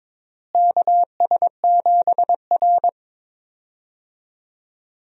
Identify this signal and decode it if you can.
Morse code, heard as a keyed tone, about 22 wpm, 700 Hz. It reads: KS7R
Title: USA Call Signs - 4 Characters - Repeated 3x - 22wpm